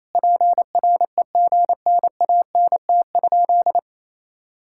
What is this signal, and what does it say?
Morse code, heard as a keyed tone, about 28 wpm, 700 Hz. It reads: PREGNANT?
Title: Sentences from Top 1000 Words - Review Part 6 - 28wpm